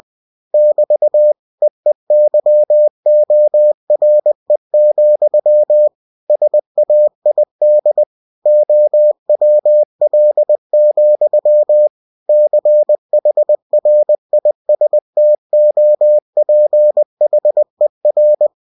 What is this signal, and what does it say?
Morse code, heard as a keyed tone, about 20 wpm, 600 Hz. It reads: <BT> EEYORE, SAID OWL, CHRISTOPHER